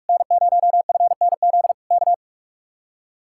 Morse code, heard as a keyed tone, about 59 wpm, spaced as for 52 wpm, 700 Hz. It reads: N0FNZ K
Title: Speed-Racing Common QSO Phrases Encore - 45wpm